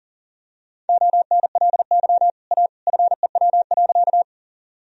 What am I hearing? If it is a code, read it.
Morse code, heard as a keyed tone, about 40 wpm, 700 Hz. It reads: ONLY A FEW.
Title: Sentences from Top 1000 Words - Review Part 1 - 40wpm